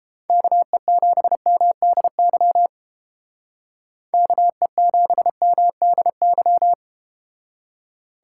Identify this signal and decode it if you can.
Morse code, heard as a keyed tone, about 31 wpm, 700 Hz. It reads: KE7MDY KE7MDY